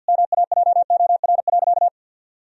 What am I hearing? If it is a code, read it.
Morse code, heard as a keyed tone, about 50 wpm, 700 Hz. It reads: MAJOR.
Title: Sentences from Top 1000 Words - Review Part 4 - 50wpm